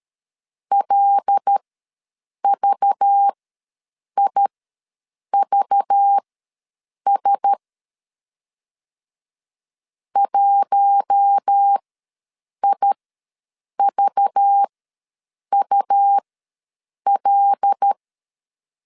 Morse code, heard as a keyed tone, about 13 wpm, spaced as for 7 wpm, 800 Hz. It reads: LVIVS 1IVUL
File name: LEZIONI / 5 LBVC / 5 LBV ws 12wpm 3015